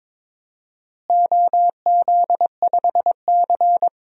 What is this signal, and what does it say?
Morse code, heard as a keyed tone, about 21 wpm, 700 Hz. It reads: OZ5C
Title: DXCC-100 Call Signs - 4 Characters - Repeated 2x - 22wpm